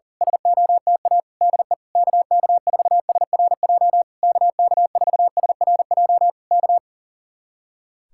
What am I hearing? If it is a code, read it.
Morse code, heard as a keyed tone, about 37 wpm, 700 Hz. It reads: SOTA DE KK4SRJ KK4SRJ K
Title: Speed-Racing Common QSO Phrases - 40wpm